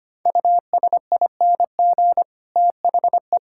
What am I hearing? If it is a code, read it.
Morse code, heard as a keyed tone, about 25 wpm, 700 Hz. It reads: USING THE
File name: practice / 25wpm / All-Age Appropriate Jokes - 25wpm